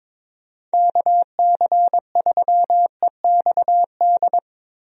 Morse code, heard as a keyed tone, about 22 wpm, 700 Hz. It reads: KC3EXD
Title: USA Call Signs - 6 Characters - 22wpm